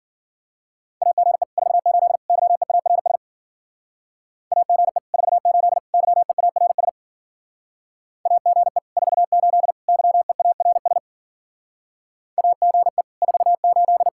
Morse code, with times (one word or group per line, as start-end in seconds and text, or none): 1.0-1.4 AGE
1.6-2.2 48
2.3-3.2 YEARS
4.5-5.0 AGE
5.1-5.8 48
5.9-6.9 YEARS
8.3-8.8 AGE
9.0-9.7 48
9.9-11.0 YEARS
12.4-13.0 AGE
13.2-14.1 48